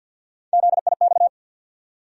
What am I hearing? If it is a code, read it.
Morse code, heard as a keyed tone, about 53 wpm, 700 Hz. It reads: ZIX
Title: USA Call Sign Postfixes - 3 Letters 50wpm